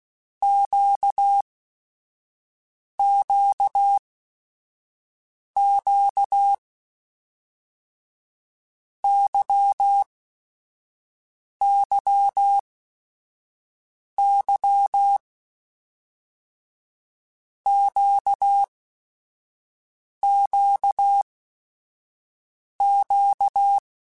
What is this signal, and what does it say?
Morse code, heard as a keyed tone, about 16 wpm, 800 Hz. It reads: Q Q Q Y Y Y Q Q Q
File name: Basic Sets / Basic Letter Groups Collective7 / LFCYQ@5